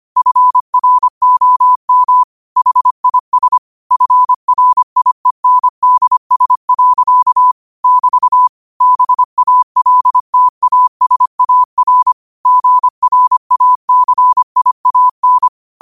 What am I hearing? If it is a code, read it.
Morse code, heard as a keyed tone, about 25 wpm, 1000 Hz. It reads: FROM HIS FRIENDS. <BT> BALTASAR GRACIAN